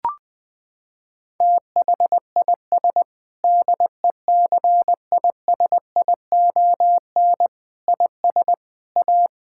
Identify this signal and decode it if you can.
Morse code, heard as a keyed tone, about 20 wpm, 700 Hz. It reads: THIS DECISION IS A
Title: Sentences with 8-Word Phrases - 20wpm